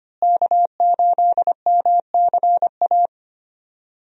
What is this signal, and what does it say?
Morse code, heard as a keyed tone, about 25 wpm, 700 Hz. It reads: K8MCA
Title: USA Call Signs - 5 Characters - Repeated 2x - 25wpm